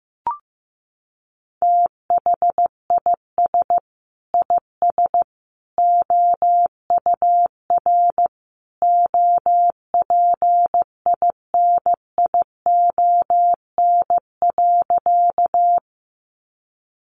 Morse code, heard as a keyed tone, about 15 wpm, 700 Hz. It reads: THIS IS OUR OPINION.
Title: Sentences from Top 1200 Words 15wpm